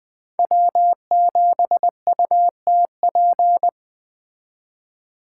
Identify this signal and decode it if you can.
Morse code, heard as a keyed tone, about 21 wpm, 700 Hz. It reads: W7UTP
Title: Speed Racing - USA 1x3 Call Signs 17wpm